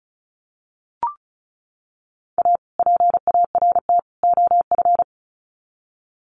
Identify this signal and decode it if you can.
Morse code, heard as a keyed tone, about 34 wpm, 700 Hz. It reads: A PART OF
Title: Sets of 3 Words 35wpm